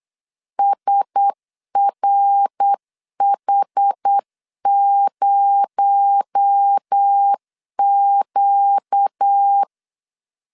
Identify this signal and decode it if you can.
Morse code, heard as a keyed tone, about 8 wpm, 800 Hz. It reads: SRH0Q
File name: LEZIONI / 9 286 8wpm 3425